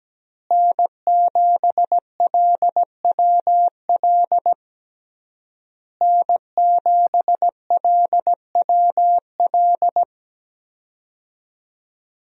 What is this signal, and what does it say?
Morse code, heard as a keyed tone, about 17 wpm, 700 Hz. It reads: N7LWL N7LWL
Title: USA Call Signs - 5 Characters - Repeated 3x - 17wpm